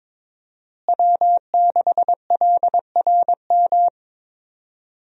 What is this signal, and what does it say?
Morse code, heard as a keyed tone, about 23 wpm, 700 Hz. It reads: W6LRM
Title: USA Call Signs - 5 Characters - Repeated 3x - 22wpm